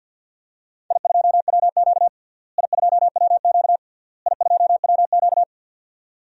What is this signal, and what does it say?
Morse code, heard as a keyed tone, about 52 wpm, 700 Hz. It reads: I2WQ I2WQ I2WQ
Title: DXCC-100 Call Signs - 4 Characters - Repeated 3x - 50wpm